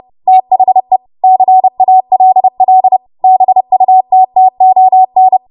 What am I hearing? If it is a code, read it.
Morse code, heard as a keyed tone, about 30 wpm, 750 Hz. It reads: THE CALL BUTTON
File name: Morse / 30WPM